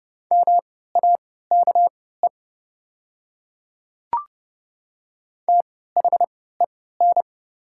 Morse code, heard as a keyed tone, about 32 wpm, spaced as for 17 wpm, 700 Hz. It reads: MAKE THEN
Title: 3x Character Spacing - Top 100 Words - 30wpm